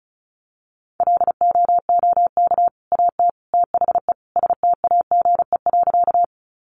Morse code, heard as a keyed tone, about 34 wpm, 700 Hz. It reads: LOOK AT THE STAGE.
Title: Sentences from Top 1200 Words 35wpm